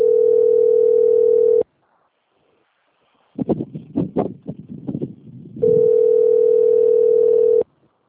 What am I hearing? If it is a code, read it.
Ringback tone (North America), 440 + 480 Hz.